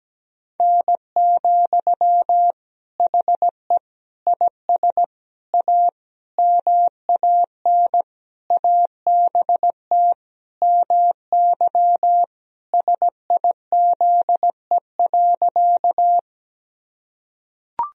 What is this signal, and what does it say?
Morse code, heard as a keyed tone, about 17 wpm, 700 Hz. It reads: N, HE IS A MAN ABT MY SIZE.
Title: Sentences from Top 400 Words and Abbreviations 17wpm